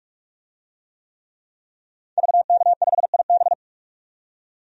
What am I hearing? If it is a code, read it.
Morse code, heard as a keyed tone, about 42 wpm, 700 Hz. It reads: VK5IB
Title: DXCC-100 Call Signs - 5 Characters - 45wpm